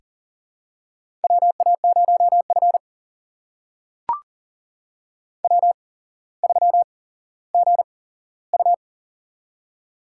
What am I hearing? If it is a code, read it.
Morse code, heard as a keyed tone, about 40 wpm, spaced as for 17 wpm, 700 Hz. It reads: WA0F W3GU
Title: USA Call Signs - 4 Characters - 8x Character Spacing - 40wpm